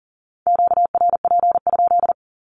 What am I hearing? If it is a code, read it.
Morse code, heard as a keyed tone, about 40 wpm, 700 Hz. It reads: QRP?
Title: Q-Codes 40wpm